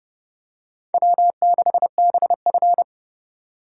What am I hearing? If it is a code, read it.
Morse code, heard as a keyed tone, about 29 wpm, 700 Hz. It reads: W6BF